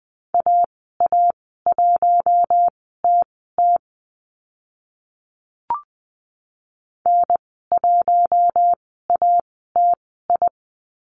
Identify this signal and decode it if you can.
Morse code, heard as a keyed tone, about 20 wpm, spaced as for 14 wpm, 700 Hz. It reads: AA1TT N1ATI
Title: 042 Lesson 8g - Number 1 - Call Signs 20wpm